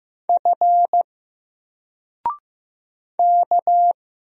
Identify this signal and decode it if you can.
Morse code, heard as a keyed tone, about 15 wpm, 700 Hz. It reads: F K
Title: Single Letter-Number Alt 15wpm